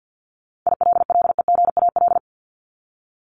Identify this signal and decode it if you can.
Morse code, heard as a keyed tone, about 48 wpm, 700 Hz. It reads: ILLEGAL
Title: Top 1751-2000 Words 50wpm